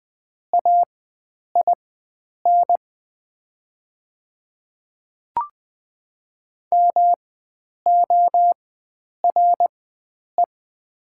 Morse code, heard as a keyed tone, about 20 wpm, spaced as for 9 wpm, 700 Hz. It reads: AIN MORE
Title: 4x Character Spacing - Top 500 Words - 20wpm